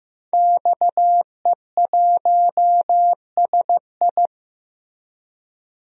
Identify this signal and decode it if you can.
Morse code, heard as a keyed tone, about 15 wpm, 700 Hz. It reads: XE1SI
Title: DXCC-100 Call Signs - 5 Characters - Repeated 3x - 15wpm